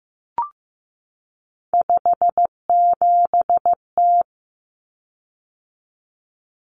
Morse code, heard as a keyed tone, about 15 wpm, 700 Hz. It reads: 57T
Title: Common Cut Numbers - 3-Digit 15wpm